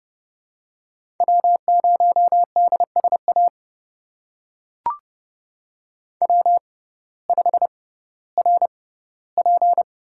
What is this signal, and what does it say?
Morse code, heard as a keyed tone, about 30 wpm, spaced as for 16 wpm, 700 Hz. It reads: W0DSA W5RP
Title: USA Call Signs - 5 Characters - 6x Character Spacing - 30wpm